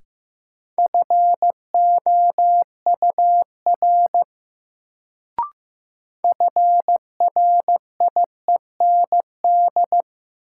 Morse code, heard as a keyed tone, about 15 wpm, 700 Hz. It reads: FOUR FRIEND
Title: Sentences from Top 300 Words - New Word Drill - 15wpm